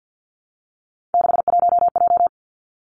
Morse code, heard as a keyed tone, about 50 wpm, 700 Hz. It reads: 61J
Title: Three Challenging Letter-Number Combinations 50wpm